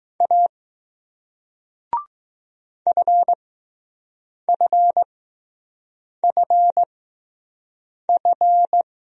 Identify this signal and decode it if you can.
Morse code, heard as a keyed tone, about 20 wpm, spaced as for 6 wpm, 700 Hz. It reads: A FFFF